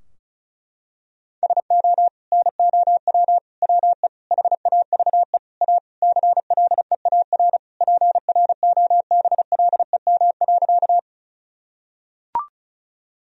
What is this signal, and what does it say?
Morse code, heard as a keyed tone, about 36 wpm, 700 Hz. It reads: SO NOW WE HAVE A CLEAR PROBLEM.
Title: Sentences from Top 1000 Words - Review Part 2 - 35wpm